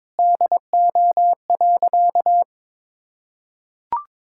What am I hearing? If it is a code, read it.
Morse code, heard as a keyed tone, about 23 wpm, 700 Hz. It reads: DO.
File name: practice / 22wpm / Sentences with 2-Word Phrases - 22wpm